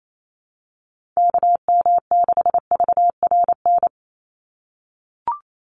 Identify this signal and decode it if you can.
Morse code, heard as a keyed tone, about 28 wpm, 700 Hz. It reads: KM6VRN
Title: USA Call Signs 28wpm